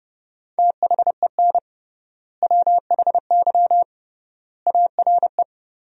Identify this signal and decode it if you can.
Morse code, heard as a keyed tone, about 30 wpm, 700 Hz. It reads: THEN WHY ARE
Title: 3x Word-Spacing - Sentences from Top 200 Words - 30wpm